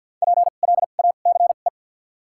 Morse code, heard as a keyed tone, about 54 wpm, spaced as for 37 wpm, 700 Hz. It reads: PLACE
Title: 2x Character Spacing - Top 200 Words - 50wpm